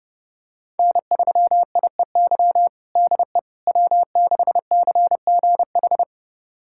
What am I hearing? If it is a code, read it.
Morse code, heard as a keyed tone, about 30 wpm, 700 Hz. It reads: N3IEY DE W6CGH
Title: USA Callsign DE Callsign - 5 Characters - 30wpm